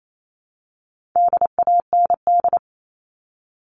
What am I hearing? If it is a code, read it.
Morse code, heard as a keyed tone, about 28 wpm, 700 Hz. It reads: DAND
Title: All Cut Numbers - 4-Digit 28wpm